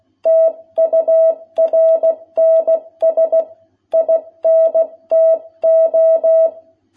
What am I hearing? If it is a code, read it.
Morse code, heard as a keyed tone, about 16 wpm, 660 Hz. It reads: TURNS INTO
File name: jh8cbh / 2021cw